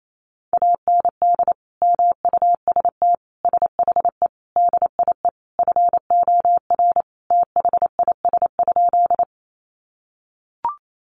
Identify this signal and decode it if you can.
Morse code, heard as a keyed tone, about 28 wpm, 700 Hz. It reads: AND MUST SHE DIE FOR THIS?
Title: Sentences from Top 1300 Words 28wpm